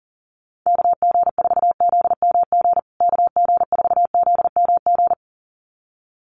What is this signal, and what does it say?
Morse code, heard as a keyed tone, about 40 wpm, 700 Hz. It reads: KG4ZMG KG4ZMG
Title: USA Call Signs Repeated 2x 40wpm